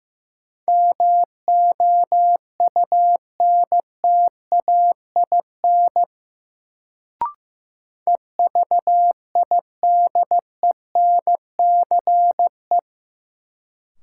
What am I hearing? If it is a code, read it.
Morse code, heard as a keyed tone, about 16 wpm, 700 Hz. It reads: MOUNTAIN EVIDENCE
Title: Sentences from Top 1300 Words - New Word Drill Alt - 15wpm